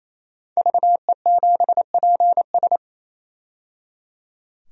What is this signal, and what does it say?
Morse code, heard as a keyed tone, about 28 wpm, 700 Hz. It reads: VE7PS
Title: DXCC-100 Call Signs - 5 Characters - 28wpm